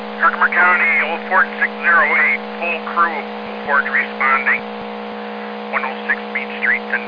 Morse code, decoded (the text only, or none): none